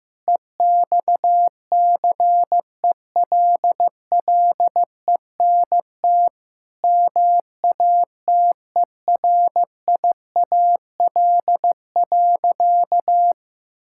Morse code, heard as a keyed tone, about 15 wpm, 700 Hz. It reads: EXCELLENT MATERIAL.
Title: Sentences from Top 1000 Words - Review Part 5 - 15wpm